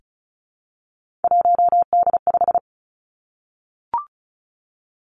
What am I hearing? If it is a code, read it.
Morse code, heard as a keyed tone, about 36 wpm, 700 Hz. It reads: 1D5
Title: Three Challenging Letter-Number Combinations 35wpm